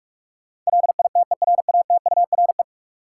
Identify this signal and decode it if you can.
Morse code, heard as a keyed tone, about 46 wpm, 700 Hz. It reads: LITERATURE
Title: Sentences from Top 600 Words - New Word Drill - 45wpm